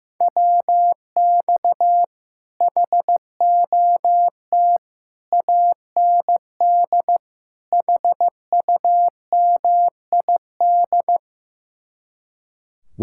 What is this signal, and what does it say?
Morse code, heard as a keyed tone, about 15 wpm, 700 Hz. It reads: WX HOT AND HUMID
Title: Speed-Racing Common QSO Phrases Encore - 15wpm